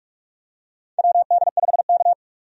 Morse code, heard as a keyed tone, about 46 wpm, 700 Hz. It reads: WD5K
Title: USA Call Signs - 4 Characters - Repeated 2x - 45wpm